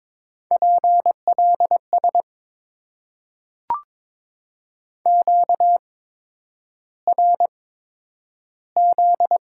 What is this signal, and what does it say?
Morse code, heard as a keyed tone, about 22 wpm, spaced as for 9 wpm, 690 Hz. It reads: PLS QRZ
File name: Top 90 QSO Elements - 8x Character Spacing - 22wpm